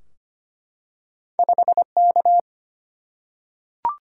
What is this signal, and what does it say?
Morse code, heard as a keyed tone, about 25 wpm, 700 Hz. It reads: 5K
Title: Single-Letter Single-Number 25wpm